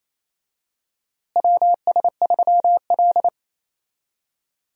Morse code, heard as a keyed tone, about 27 wpm, 700 Hz. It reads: WS3L
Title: USA Call Signs - 4 Characters - Repeated 2x - 28wpm